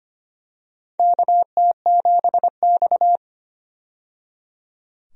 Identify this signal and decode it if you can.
Morse code, heard as a keyed tone, about 25 wpm, 700 Hz. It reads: KT7X
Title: USA Call Signs - 4 Characters - 25wpm